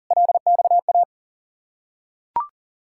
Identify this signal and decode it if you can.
Morse code, heard as a keyed tone, about 40 wpm, 700 Hz. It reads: LXA
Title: Ontario QSO Party Exchange 40wpm